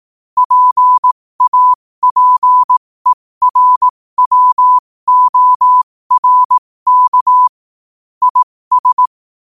Morse code, heard as a keyed tone, about 18 wpm, spaced as for 15 wpm, 1000 Hz. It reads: PAPERWORK IS
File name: qotdfiles / 15wpm